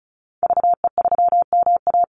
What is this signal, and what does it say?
Morse code, heard as a keyed tone, about 34 wpm, 700 Hz. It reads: VE3MA
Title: Call Signs - 35wpm